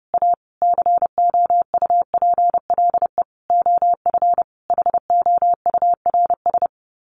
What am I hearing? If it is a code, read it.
Morse code, heard as a keyed tone, about 30 wpm, 700 Hz. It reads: A COUPLE OF HOURS
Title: Sets of 5 Words Encore2 30wpm